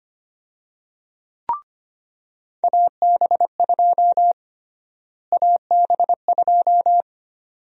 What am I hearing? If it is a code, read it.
Morse code, heard as a keyed tone, about 25 wpm, 700 Hz. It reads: AB2 AB2
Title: USA Call Sign Prefixes - Repeated 2x - 25wpm